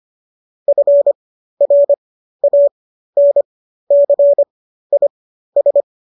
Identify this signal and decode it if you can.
Morse code, heard as a keyed tone, about 25 wpm, spaced as for 13 wpm, 570 Hz. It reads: FRANCIS